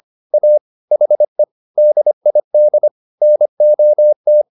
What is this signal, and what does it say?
Morse code, heard as a keyed tone, about 25 wpm, 600 Hz. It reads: A HE DID NOT